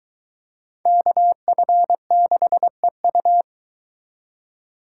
Morse code, heard as a keyed tone, about 23 wpm, 700 Hz. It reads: KF6EU